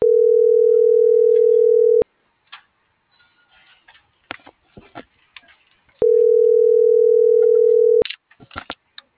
Ringback tone (North America), 440 + 480 Hz.